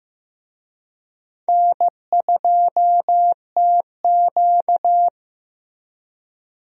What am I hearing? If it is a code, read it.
Morse code, heard as a keyed tone, about 15 wpm, 700 Hz. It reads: N2TQ
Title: USA Call Signs - 4 Characters - Repeated 2x - 15wpm